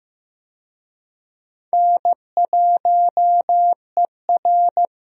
Morse code, heard as a keyed tone, about 15 wpm, 700 Hz. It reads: N1ER
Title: USA Call Signs - 4 Characters - Repeated 3x - 15wpm